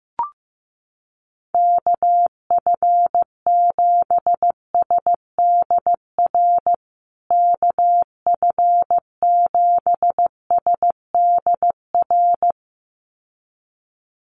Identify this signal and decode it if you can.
Morse code, heard as a keyed tone, about 15 wpm, 700 Hz. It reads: KF7SDR KF7SDR
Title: USA Call Signs Repeated 2x 15wpm